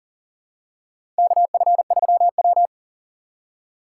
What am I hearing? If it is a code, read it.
Morse code, heard as a keyed tone, about 40 wpm, 700 Hz. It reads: KF3W
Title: USA Call Signs - 4 Characters - 6x Character Spacing - 40wpm